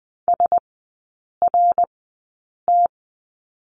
Morse code, heard as a keyed tone, about 20 wpm, spaced as for 8 wpm, 700 Hz. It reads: SRT
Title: 053 Lesson 10e - Letter r - Three Characters 20wpm